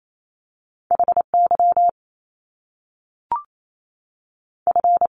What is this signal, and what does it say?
Morse code, heard as a keyed tone, about 28 wpm, spaced as for 3 wpm, 700 Hz. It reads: H YF
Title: Top Two Letter Combinations in Words 28wpm